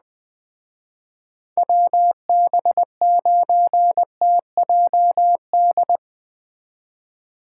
Morse code, heard as a keyed tone, about 20 wpm, 700 Hz. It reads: WB9TJD
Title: Speed Racing - USA Call Signs 17wpm